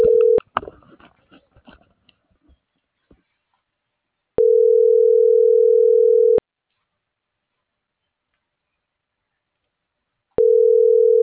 Ringback tone (North America), 440 + 480 Hz.